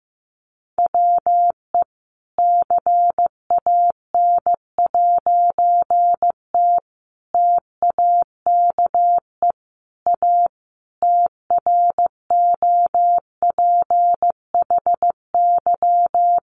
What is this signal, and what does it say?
Morse code, heard as a keyed tone, about 15 wpm, 700 Hz. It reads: WE CAN'T TAKE A TROPHY
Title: A Sound Of Thunder 15wpm